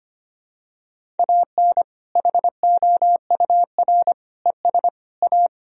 Morse code, heard as a keyed tone, about 25 wpm, 700 Hz. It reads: AN HOUR ES A
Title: Sentences from Top 400 Words and Abbreviations 25wpm